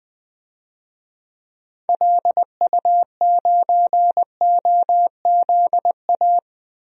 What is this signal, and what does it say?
Morse code, heard as a keyed tone, about 20 wpm, 700 Hz. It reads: LU9OZA
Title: DXCC-100 Call Signs - 6 Characters - 20wpm